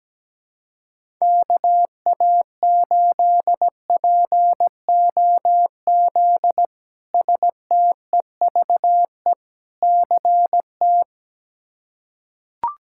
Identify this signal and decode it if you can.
Morse code, heard as a keyed tone, about 17 wpm, 700 Hz. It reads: KA8POZ STEVE CT
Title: K1USN SST QSO Practice - 17wpm